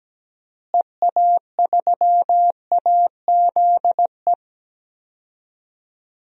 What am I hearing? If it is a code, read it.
Morse code, heard as a keyed tone, about 17 wpm, 700 Hz. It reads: EA3AZE
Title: DXCC-100 Call Signs - 6 Characters - Repeated 3x - 17wpm